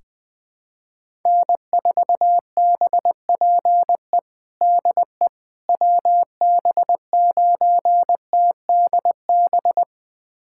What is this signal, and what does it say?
Morse code, heard as a keyed tone, about 20 wpm, 700 Hz. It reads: N4BPE DE WB9TDB
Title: USA Callsign DE Callsign 20wpm